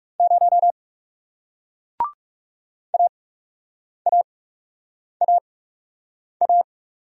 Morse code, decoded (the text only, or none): 0 AAAA